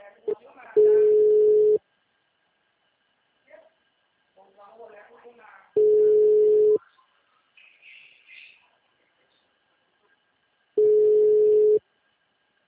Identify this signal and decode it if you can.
Ringback tone (Europe, 425 Hz), with speech.
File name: out- -2030- - - .40945